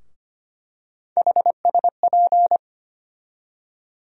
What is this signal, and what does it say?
Morse code, heard as a keyed tone, about 25 wpm, 700 Hz. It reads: HSP
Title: USA Call Sign Postfixes - 3 Letters 25wpm